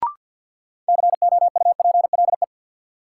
Morse code, heard as a keyed tone, about 53 wpm, 700 Hz. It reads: COUPLE
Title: Sets of 2 Words - New Word Drill - 50wpm